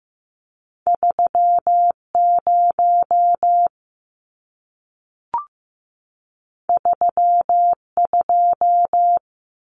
Morse code, heard as a keyed tone, about 15 wpm, 700 Hz. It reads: 3 0 3 2
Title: Numbers - 2-Digit 15wpm